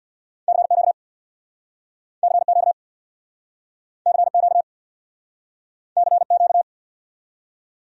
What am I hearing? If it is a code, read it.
Morse code, heard as a keyed tone, about 69 wpm, 700 Hz. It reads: CQ CQ CQ CQ